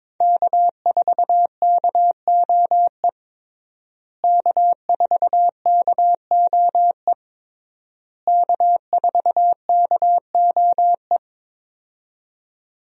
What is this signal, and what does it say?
Morse code, heard as a keyed tone, about 22 wpm, 700 Hz. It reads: K4KOE K4KOE K4KOE
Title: USA Call Signs - 5 Characters - Repeated 3x - 22wpm